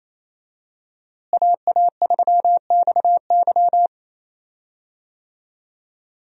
Morse code, heard as a keyed tone, about 28 wpm, 700 Hz. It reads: AA3XY
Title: USA Call Signs - 5 Characters - Repeated 2x - 28wpm